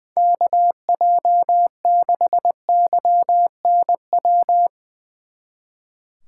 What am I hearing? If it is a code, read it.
Morse code, heard as a keyed tone, about 22 wpm, 700 Hz. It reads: KJ6YNW
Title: Speed Racing - USA Call Signs 20wpm